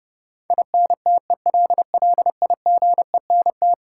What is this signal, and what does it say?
Morse code, heard as a keyed tone, about 29 wpm, 700 Hz. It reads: INTELLIGENT
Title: Sentences from Top 1000 Words - Review Part 7 - New Word Drill - 30wpm